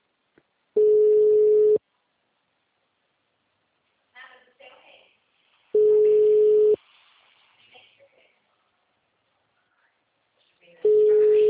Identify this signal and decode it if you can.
Ringback tone (Europe, 425 Hz), with speech.